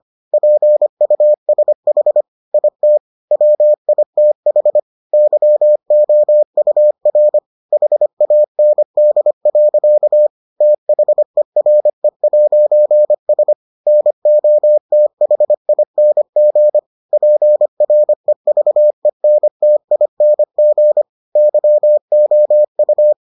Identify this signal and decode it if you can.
Morse code, heard as a keyed tone, about 25 wpm, 600 Hz. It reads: PUSH IT WITH YOUR HAND. THERE'S NOTHING PREVENTING YOU